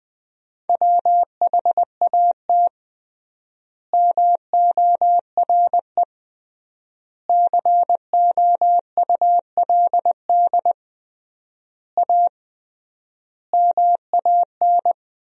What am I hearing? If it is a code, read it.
Morse code, heard as a keyed tone, about 20 wpm, 700 Hz. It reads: WHAT MORE COULD A MAN
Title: 3x Word-Spacing - Sentences from Top 200 Words - 20wpm